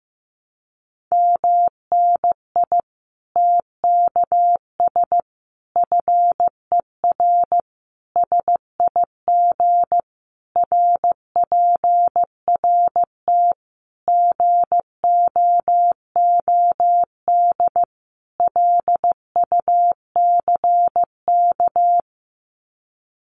Morse code, heard as a keyed tone, about 15 wpm, 700 Hz. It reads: MNI TKS FER SIG RPRT GOOD LUCK
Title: Common QSO Phrases 15wpm